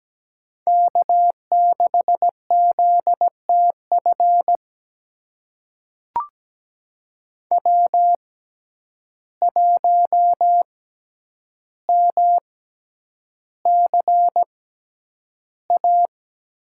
Morse code, heard as a keyed tone, about 17 wpm, spaced as for 9 wpm, 700 Hz. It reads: K6ZTF W1MCA
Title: USA Call Signs - 5 Characters - 6x Character Spacing - 17wpm